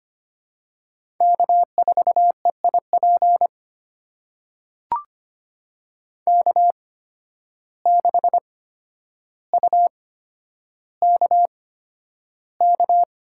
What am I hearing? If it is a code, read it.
Morse code, heard as a keyed tone, about 25 wpm, spaced as for 11 wpm, 700 Hz. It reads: K4EIP K6UKK
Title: USA Call Signs - 5 Characters - 8x Character Spacing - 25wpm